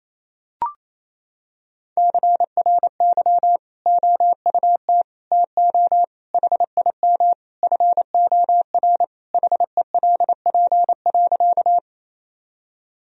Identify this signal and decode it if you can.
Morse code, heard as a keyed tone, about 28 wpm, 700 Hz. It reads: CRY OUT TO HIM FOR HELP.
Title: Sentences from Top 1000 Words - Review Part 2 - 28wpm